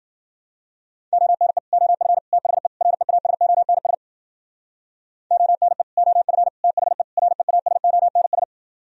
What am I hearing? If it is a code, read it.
Morse code, heard as a keyed tone, about 55 wpm, 700 Hz. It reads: ONE OF THE REASONS ONE OF THE REASONS